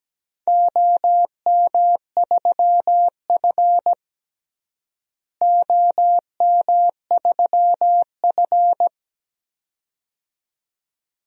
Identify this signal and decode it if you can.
Morse code, heard as a keyed tone, about 17 wpm, 700 Hz. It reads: OM3F OM3F
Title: DXCC-100 Call Signs - 4 Characters - Repeated 2x - 17wpm